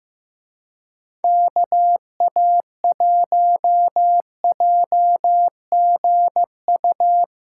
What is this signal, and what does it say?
Morse code, heard as a keyed tone, about 17 wpm, 700 Hz. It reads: KA1JGU
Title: Speed Racing - USA 2x3 Call Signs 15wpm